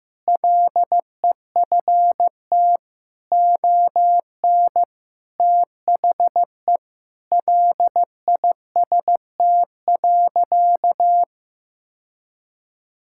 Morse code, heard as a keyed tone, about 15 wpm, 700 Hz. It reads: LEFT ON THE LIST.
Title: Sentences from Top 1000 Words - Review Part 6 - 15wpm